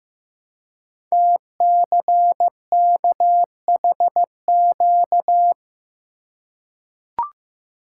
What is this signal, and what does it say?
Morse code, heard as a keyed tone, about 15 wpm, 700 Hz. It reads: TCKHQ
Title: Code Groups - 5 Letters - 15wpm